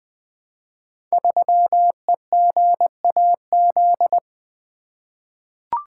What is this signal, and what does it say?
Morse code, heard as a keyed tone, about 20 wpm, 700 Hz. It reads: 3EGAZ
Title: Code Groups - 5 Letter:Numbers - 20wpm